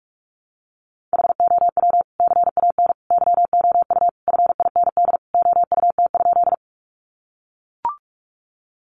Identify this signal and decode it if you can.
Morse code, heard as a keyed tone, about 44 wpm, 700 Hz. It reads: HOW CAN YOU FIND OUT?
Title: Sentences from Top 200 Words 45wpm